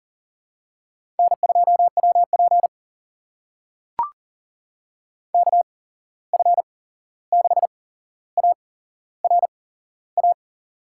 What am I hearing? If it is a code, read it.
Morse code, heard as a keyed tone, about 38 wpm, spaced as for 15 wpm, 700 Hz. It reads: N2WP KF6ARA
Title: USA Call Signs - 8x Character Spacing - 40wpm